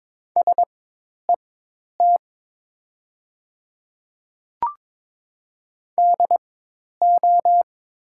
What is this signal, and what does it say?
Morse code, heard as a keyed tone, about 22 wpm, spaced as for 10 wpm, 700 Hz. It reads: SET DO
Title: 4x Character Spacing - Top 200 Words - 22wpm